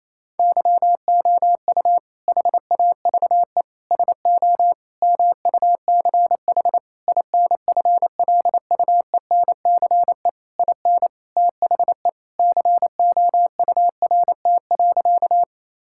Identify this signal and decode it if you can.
Morse code, heard as a keyed tone, about 28 wpm, 700 Hz. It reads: YOU HAVE SO MUCH INFLUENCE IN THE COURT.
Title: Sentences from Top 1600 Words 28wpm